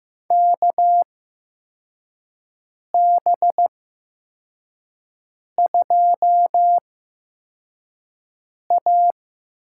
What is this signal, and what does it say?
Morse code, heard as a keyed tone, about 15 wpm, spaced as for 4 wpm, 700 Hz. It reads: KB2A
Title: USA Call Signs - 6 Characters - 8x Character Spacing - 15wpm